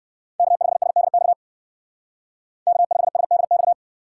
Morse code, heard as a keyed tone, about 58 wpm, 700 Hz. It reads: K5SDX K5SDX